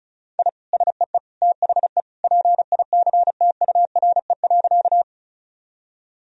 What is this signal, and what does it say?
Morse code, heard as a keyed tone, about 36 wpm, 700 Hz. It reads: I SEE THE PICTURE.
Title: Sentences from Top 1000 Words - Review Part 1 - 35wpm